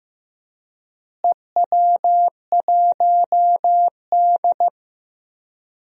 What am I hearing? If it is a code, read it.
Morse code, heard as a keyed tone, about 15 wpm, 700 Hz. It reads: EW1D
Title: DXCC-100 Call Signs - 4 Characters - Repeated 3x - 15wpm